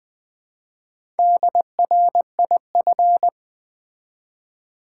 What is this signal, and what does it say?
Morse code, heard as a keyed tone, about 20 wpm, 700 Hz. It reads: DRIF
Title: Build A Word - 1 Letter At A Time - 3 to 5-Letter Words - 20wpm